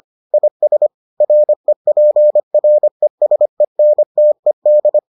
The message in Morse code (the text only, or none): IS REPRESENTED